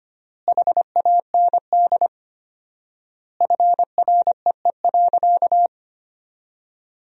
Morse code, heard as a keyed tone, about 25 wpm, 700 Hz. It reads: HAND FREE.
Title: 4x Word-Spacing - Sentences from Top 500 Words - 25wpm